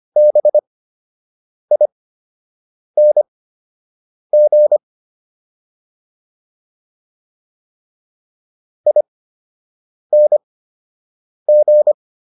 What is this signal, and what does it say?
Morse code, heard as a keyed tone, about 25 wpm, spaced as for 7 wpm, 600 Hz. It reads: BING ING